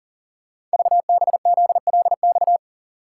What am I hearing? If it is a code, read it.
Morse code, heard as a keyed tone, about 40 wpm, 690 Hz. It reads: VBZLX
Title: Code Groups - 5 Letters - 40wpm